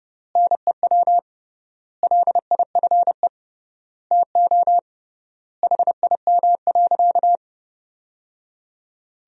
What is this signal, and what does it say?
Morse code, heard as a keyed tone, about 32 wpm, 700 Hz. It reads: NEW LIFE TO HIM.
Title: 3x Word-Spacing - Sentences from Top 300 Words - 30wpm